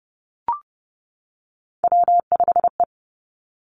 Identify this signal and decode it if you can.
Morse code, heard as a keyed tone, about 30 wpm, 700 Hz. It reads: W5E
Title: Three Challenging Letter-Number Combinations 30wpm